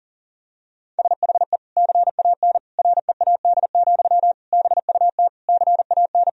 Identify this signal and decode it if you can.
Morse code, heard as a keyed tone, about 40 wpm, 700 Hz. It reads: SHE CAN READ, BUT CAN
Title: Sentences from Top 1000 Words - Review Part 1 - 40wpm